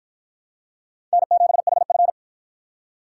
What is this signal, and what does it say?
Morse code, heard as a keyed tone, about 53 wpm, 700 Hz. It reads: N7HF